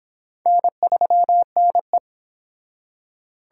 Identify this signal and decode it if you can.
Morse code, heard as a keyed tone, about 24 wpm, 700 Hz. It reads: N3NE